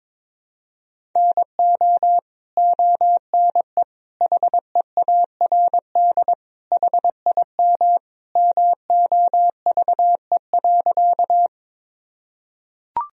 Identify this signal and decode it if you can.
Morse code, heard as a keyed tone, about 21 wpm, 700 Hz. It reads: NO ONE HEARD HIM MOVE.
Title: Sentences from Top 1000 Words - Review Part 1 - 22wpm